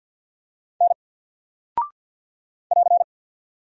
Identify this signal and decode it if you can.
Morse code, heard as a keyed tone, about 48 wpm, 700 Hz. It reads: N <AR>
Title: Single Character (Letter:Number:Punctuation:Prosign) Alt 50wpm